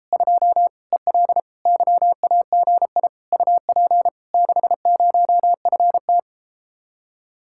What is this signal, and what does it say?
Morse code, heard as a keyed tone, about 33 wpm, 700 Hz. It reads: 2 EL YAGI UP 60FT